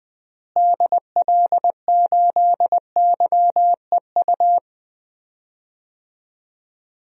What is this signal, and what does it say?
Morse code, heard as a keyed tone, about 20 wpm, 700 Hz. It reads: DL8YEU
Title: DXCC-100 Call Signs - 6 Characters - Repeated 2x - 20wpm